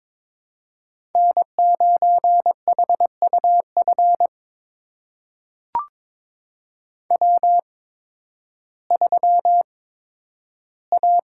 Morse code, heard as a keyed tone, about 22 wpm, spaced as for 12 wpm, 700 Hz. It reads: N9HUF W3A
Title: USA Call Signs - 8x Character Spacing - 22wpm